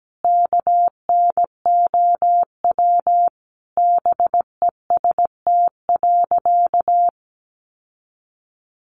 Morse code, heard as a keyed tone, about 17 wpm, 700 Hz. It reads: KNOW BEST.
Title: Sentences from Top 600 Words 17wpm